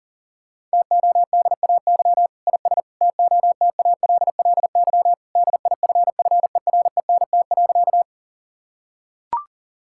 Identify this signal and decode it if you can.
Morse code, heard as a keyed tone, about 40 wpm, 700 Hz. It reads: TODAY IS TOTALLY DIFFERENT.
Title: Sentences from Top 1600 Words 40wpm